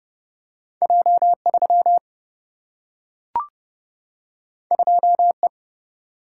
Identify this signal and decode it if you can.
Morse code, heard as a keyed tone, about 30 wpm, 700 Hz. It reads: J 3 2 E
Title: Single-Letter Single-Number 30wpm